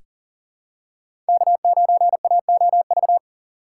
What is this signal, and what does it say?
Morse code, heard as a keyed tone, about 40 wpm, 700 Hz. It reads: K9AOV
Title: USA Call Signs - 4x Character Spacing - 40wpm